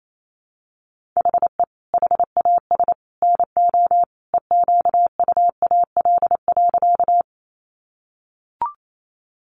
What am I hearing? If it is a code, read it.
Morse code, heard as a keyed tone, about 28 wpm, 700 Hz. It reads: HE HAS NO EQUAL.
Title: Sentences from Top 1400 Words 28wpm